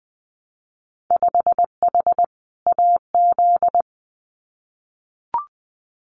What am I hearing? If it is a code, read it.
Morse code, heard as a keyed tone, about 19 wpm, 700 Hz. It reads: 5H AZ
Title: Winter Field Day Exchanges Encore 20wpm